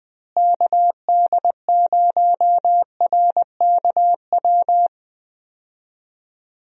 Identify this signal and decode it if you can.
Morse code, heard as a keyed tone, about 20 wpm, 700 Hz. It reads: KD0RKW
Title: USA Call Signs - 6 Characters - Repeated 3x - 20wpm